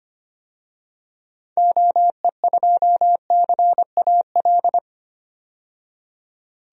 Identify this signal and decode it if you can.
Morse code, heard as a keyed tone, about 25 wpm, 690 Hz. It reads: OE2CAL
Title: DXCC-100 Call Signs - 6 Characters - 25wpm